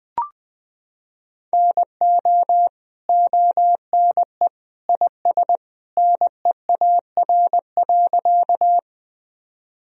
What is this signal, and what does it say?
Morse code, heard as a keyed tone, about 21 wpm, 700 Hz. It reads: NO ONE IS NEAR.
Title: Sentences from Top 1000 Words - Review Part 1 - 20wpm